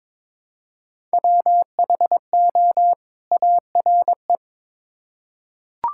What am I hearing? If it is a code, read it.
Morse code, heard as a keyed tone, about 22 wpm, 700 Hz. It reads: WHO ARE
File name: Build A Phrase - 2-Word Phrases - 22wpm